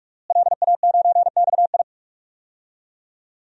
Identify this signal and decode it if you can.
Morse code, heard as a keyed tone, about 46 wpm, 700 Hz. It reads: LA9XI
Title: DXCC-100 Call Signs - 5 Characters - Repeated 3x - 45wpm